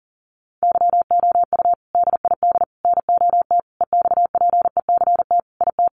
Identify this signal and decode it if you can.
Morse code, heard as a keyed tone, about 40 wpm, 700 Hz. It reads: YOU DID NOT EXPECT IT
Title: Sentences from Top 1100 Words 40wpm